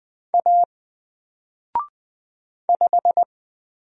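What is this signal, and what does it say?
Morse code, heard as a keyed tone, about 20 wpm, 700 Hz. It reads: A 5
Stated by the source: Single letter and number Morse drill at 20 wpm, from N4LCD.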